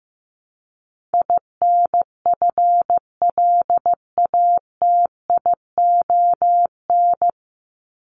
Morse code, heard as a keyed tone, about 15 wpm, 700 Hz. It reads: INFLATION
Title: Top 1001-1250 Words 15wpm